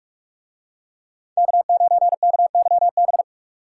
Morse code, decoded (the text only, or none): K9KYB